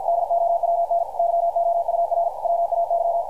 Morse code, heard as a keyed tone, about 50 wpm, 680 Hz. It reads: DOWN YOUR WAY
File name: archived / CRSnet